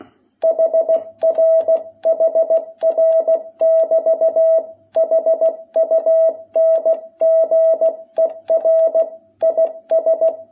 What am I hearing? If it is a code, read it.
Morse code, heard as a keyed tone, about 16 wpm, 660 Hz. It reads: HR HR<BT> HUNGER IS
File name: jh8cbh / 2021cw